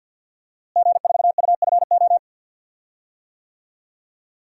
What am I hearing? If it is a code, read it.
Morse code, heard as a keyed tone, about 51 wpm, 700 Hz. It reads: G4UFO
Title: DXCC-100 Call Signs - 5 Characters - Repeated 2x - 50wpm